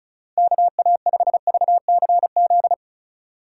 Morse code, heard as a keyed tone, about 34 wpm, 700 Hz. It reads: KA5VCZ
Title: USA Call Signs - 6x Character Spacing - 35wpm